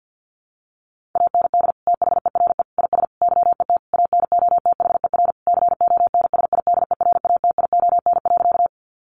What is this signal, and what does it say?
Morse code, heard as a keyed tone, about 50 wpm, 700 Hz. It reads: AND THERE IS YET ANOTHER CONSIDERATION.
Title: Sentences from Top 1500 Words 50wpm